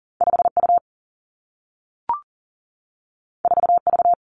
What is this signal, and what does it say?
Morse code, heard as a keyed tone, about 40 wpm, 700 Hz. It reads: 5 U 4 V
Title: Two Challenging Letter-Number Combinations 40wpm